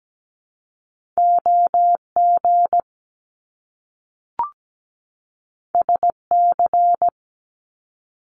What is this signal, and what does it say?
Morse code, heard as a keyed tone, about 17 wpm, 700 Hz. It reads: OG SC
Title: Top Two Letter Combinations in Words 17wpm